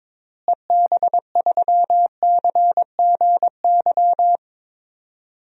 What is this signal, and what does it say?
Morse code, heard as a keyed tone, about 22 wpm, 700 Hz. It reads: EB3CGY
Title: DXCC-100 Call Signs - 6 Characters - Repeated 3x - 22wpm